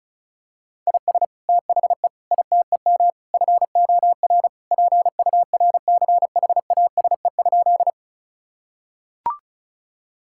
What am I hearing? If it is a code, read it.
Morse code, heard as a keyed tone, about 34 wpm, 700 Hz. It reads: IS THE ITEM FOR PURCHASE?
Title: Sentences from Top 1000 Words - Review Part 6 - 35wpm